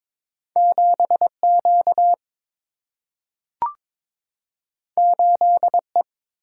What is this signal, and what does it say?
Morse code, heard as a keyed tone, about 22 wpm, 700 Hz. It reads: 7 Q 8 E
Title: Single-Letter Single-Number 22wpm